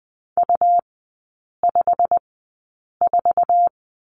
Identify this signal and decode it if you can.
Morse code, heard as a keyed tone, about 20 wpm, spaced as for 8 wpm, 700 Hz. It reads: U54
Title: 100 Lesson 17e - Letter u - Three Characters 20wpm